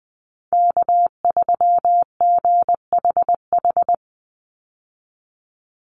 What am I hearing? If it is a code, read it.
Morse code, heard as a keyed tone, about 20 wpm, 700 Hz. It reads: K3GHH
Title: Call Signs - 20wpm